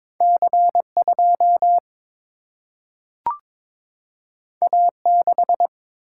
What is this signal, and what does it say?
Morse code, heard as a keyed tone, about 22 wpm, 700 Hz. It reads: C 2 A 6
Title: Single-Letter Single-Number 22wpm